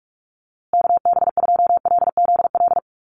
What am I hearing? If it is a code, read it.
Morse code, heard as a keyed tone, about 44 wpm, 700 Hz. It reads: KB2LZL
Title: USA Call Signs Repeated 3x 45wpm